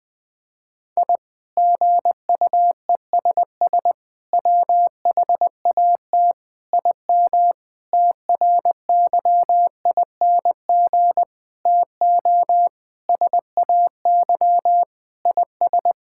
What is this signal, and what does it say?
Morse code, heard as a keyed tone, about 20 wpm, 700 Hz. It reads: I GUESS WHAT IM TRYING TO SAY IS
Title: Sentences with 8-Word Phrases - Spoken Beginning - 20wpm